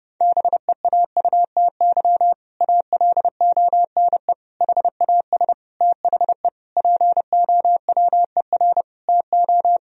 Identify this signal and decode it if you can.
Morse code, heard as a keyed tone, about 32 wpm, 700 Hz. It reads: BEAUTY ALONE HAS THE POWER TO
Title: Sentences from Top 1000 Words - Review Part 2 - 30wpm